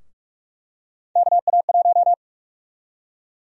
Morse code, heard as a keyed tone, about 46 wpm, 700 Hz. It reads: KA1
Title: USA Call Sign Prefixes - Repeated 2x - 45wpm